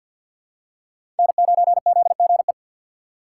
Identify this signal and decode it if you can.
Morse code, heard as a keyed tone, about 53 wpm, 700 Hz. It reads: N9ZGE
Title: USA Call Signs - 6x Character Spacing - 50wpm